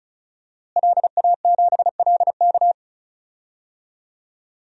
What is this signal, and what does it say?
Morse code, heard as a keyed tone, about 34 wpm, 700 Hz. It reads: LA7LK
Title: DXCC-100 Call Signs - 5 Characters - Repeated 2x - 35wpm